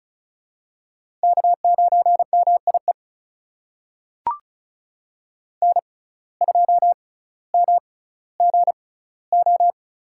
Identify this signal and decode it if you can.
Morse code, heard as a keyed tone, about 36 wpm, spaced as for 18 wpm, 700 Hz. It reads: K9MIE N2MGO
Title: USA Call Signs - 5 Characters - 6x Character Spacing - 35wpm